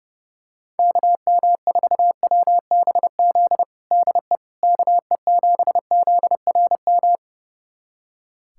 Morse code, heard as a keyed tone, about 30 wpm, 700 Hz. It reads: KM4WBZ DE KE7ZRM
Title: USA Callsign DE Callsign 30wpm